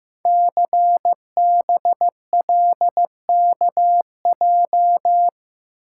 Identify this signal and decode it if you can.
Morse code, heard as a keyed tone, about 15 wpm, 700 Hz. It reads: CBLKJ
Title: Code Groups - 5 Letters - 15wpm